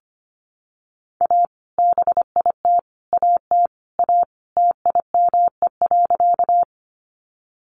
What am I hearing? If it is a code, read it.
Morse code, heard as a keyed tone, about 25 wpm, 700 Hz. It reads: A BIT AT A TIME.
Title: Sentences from Top 900 Words 25wpm